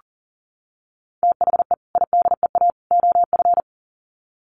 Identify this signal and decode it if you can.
Morse code, heard as a keyed tone, about 40 wpm, 700 Hz. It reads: THE IDEA OF
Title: Sets of 3 Words 40wpm